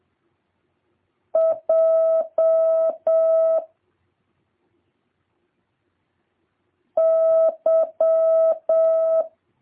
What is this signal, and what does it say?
Morse code, heard as a keyed tone, about 7 wpm, 650 Hz. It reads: J Y